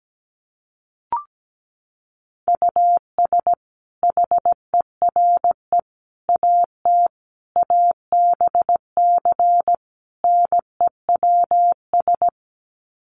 Morse code, heard as a keyed tone, about 17 wpm, 700 Hz. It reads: US HERE AT ABC NEWS
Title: Sets of 5 Words 17wpm